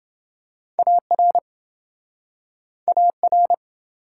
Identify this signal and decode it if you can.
Morse code, heard as a keyed tone, about 28 wpm, 700 Hz. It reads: AR AR